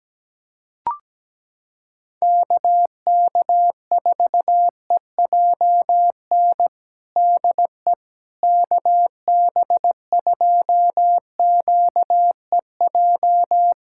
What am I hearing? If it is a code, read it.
Morse code, heard as a keyed tone, about 17 wpm, 700 Hz. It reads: KK4EJN DE KB2QEJ
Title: USA Callsign DE Callsign 17wpm